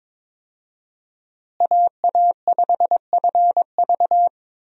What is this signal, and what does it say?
Morse code, heard as a keyed tone, about 21 wpm, 700 Hz. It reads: AA5FV
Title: USA Call Signs - 5 Characters - Repeated 2x - 22wpm